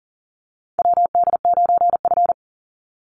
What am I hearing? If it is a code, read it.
Morse code, heard as a keyed tone, about 39 wpm, 700 Hz. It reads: WD9F
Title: NA SOTA Chaser Callsigns 40wpm